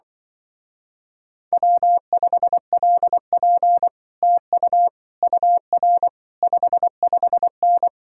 Morse code, heard as a keyed tone, about 24 wpm, 700 Hz. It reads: W5LP TU UR 55N
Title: Speed-Racing Common QSO Phrases - 20wpm